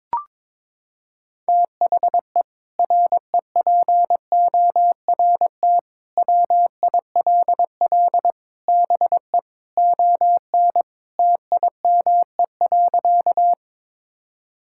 Morse code, heard as a keyed tone, about 22 wpm, 700 Hz. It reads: THE REPORT WILL BE ON TIME.
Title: Sentences from Top 1000 Words - Review Part 6 - 22wpm